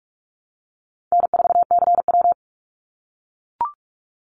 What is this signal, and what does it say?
Morse code, heard as a keyed tone, about 44 wpm, 700 Hz. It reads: N4CW
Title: Easy Call Signs - 45wpm